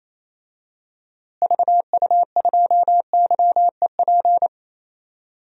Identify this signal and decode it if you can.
Morse code, heard as a keyed tone, about 28 wpm, 700 Hz. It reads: VU2YEP
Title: DXCC-100 Call Signs - 6 Characters - Repeated 2x - 28wpm